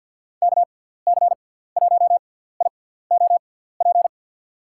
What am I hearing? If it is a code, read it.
Morse code, heard as a keyed tone, about 53 wpm, spaced as for 17 wpm, 700 Hz. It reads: KC1IOP
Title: USA Call Signs - 6 Characters - 6x Character Spacing - 50wpm